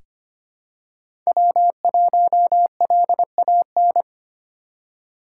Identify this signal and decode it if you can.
Morse code, heard as a keyed tone, about 25 wpm, 700 Hz. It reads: W1LAN
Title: USA Call Signs - 5 Characters - 3x Character Spacing - 25wpm